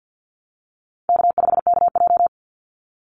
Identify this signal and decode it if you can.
Morse code, heard as a keyed tone, about 50 wpm, 700 Hz. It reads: K5KJ
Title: Call Signs Repeated 2x - 50wpm